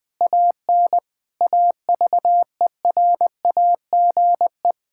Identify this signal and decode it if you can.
Morse code, heard as a keyed tone, about 20 wpm, 700 Hz. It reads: AN AVERAGE
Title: Sets of 9 Words - 20wpm